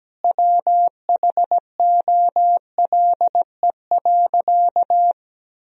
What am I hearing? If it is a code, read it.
Morse code, heard as a keyed tone, about 17 wpm, 700 Hz. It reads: WHOLE.
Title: Sentences from Top 1000 Words - Review Part 1 - 17wpm